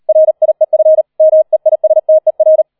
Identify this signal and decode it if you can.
Morse code, heard as a keyed tone, about 38 wpm, 630 Hz. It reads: RIEF MEISTER